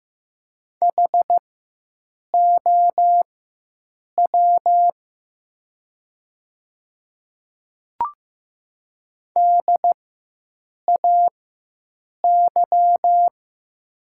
Morse code, heard as a keyed tone, about 15 wpm, spaced as for 7 wpm, 700 Hz. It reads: HOW DAY
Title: 4x Character Spacing - Top 100 Words - 15wpm